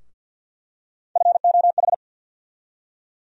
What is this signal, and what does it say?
Morse code, heard as a keyed tone, about 51 wpm, 700 Hz. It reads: FOH
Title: USA Call Sign Postfixes - 3 Letters 50wpm